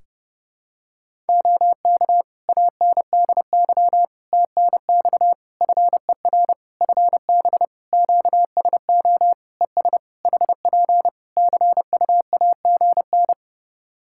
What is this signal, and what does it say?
Morse code, heard as a keyed tone, about 30 wpm, 700 Hz. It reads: OK ANDY TNX FER FB QSO ES HP CUAGN
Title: K1USN SST QSO Practice 30wpm